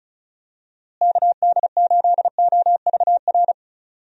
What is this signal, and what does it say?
Morse code, heard as a keyed tone, about 36 wpm, 700 Hz. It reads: KD8OVR
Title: USA Call Signs - 6 Characters - Repeated 2x - 35wpm